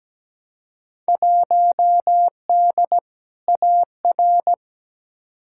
Morse code, heard as a keyed tone, about 17 wpm, 700 Hz. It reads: 1D AR
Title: Field Day Exchanges - 17wpm